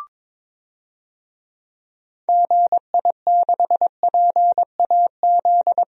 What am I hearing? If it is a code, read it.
Morse code, heard as a keyed tone, about 21 wpm, 700 Hz. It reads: GI6PAZ
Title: DXCC-100 Call Signs - 6 Characters - Repeated 3x - 22wpm